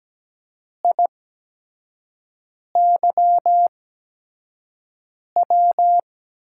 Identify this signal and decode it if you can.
Morse code, heard as a keyed tone, about 17 wpm, spaced as for 5 wpm, 700 Hz. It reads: IYW
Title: USA Call Signs - 6 Characters - 8x Character Spacing - 17wpm